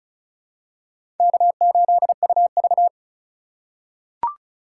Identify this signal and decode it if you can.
Morse code, heard as a keyed tone, about 36 wpm, 700 Hz. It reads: K8UV
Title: USA Call Signs - 2x Character Spacing - 35wpm